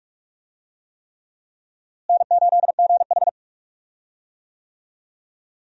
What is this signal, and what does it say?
Morse code, heard as a keyed tone, about 46 wpm, 700 Hz. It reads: N8GH
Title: USA Call Signs - 4 Characters - Repeated 3x - 45wpm